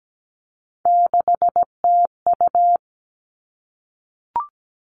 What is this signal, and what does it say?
Morse code, heard as a keyed tone, about 17 wpm, 700 Hz. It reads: 6TU
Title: All Cut Numbers - 3-Digit 17wpm